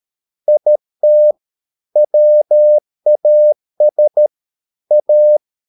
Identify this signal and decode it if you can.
Morse code, heard as a keyed tone, about 13 wpm, 600 Hz. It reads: IT WAS A